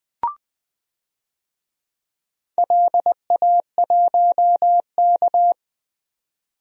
Morse code, heard as a keyed tone, about 20 wpm, 700 Hz. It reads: LA1K
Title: DXCC-100 Call Signs - 4 Characters - 20wpm